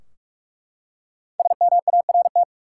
Morse code, heard as a keyed tone, about 42 wpm, 700 Hz. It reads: SMART
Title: Five Letter Words - 45wpm